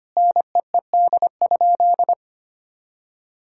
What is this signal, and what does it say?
Morse code, heard as a keyed tone, about 25 wpm, 700 Hz. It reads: NEED?
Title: Sentences from Top 1000 Words - Review Part 5 - 25wpm